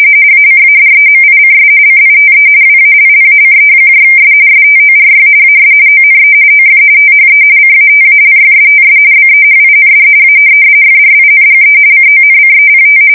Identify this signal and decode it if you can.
RTTY, about 45.45 baud. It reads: ACADEMICS CITE WALLACES STORY AS ONE OF THE CATALYSTS BEHIND A FRINGE CONCEPT
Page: 170 Shift 2125/2295